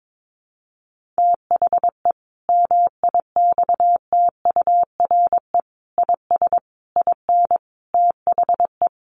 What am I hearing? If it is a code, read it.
Morse code, heard as a keyed tone, about 22 wpm, 700 Hz. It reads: THE MIXTURE IS IN THE
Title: Sentences from Top 1500 Words 22wpm